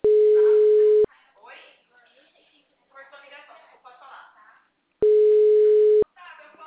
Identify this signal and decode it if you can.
Ringback tone (Europe, 425 Hz), with speech.
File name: out- -2009- - - .630